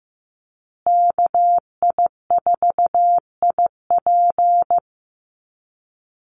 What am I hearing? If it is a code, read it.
Morse code, heard as a keyed tone, about 15 wpm, 700 Hz. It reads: KI4IP
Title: USA Call Signs Repeated 3x 15wpm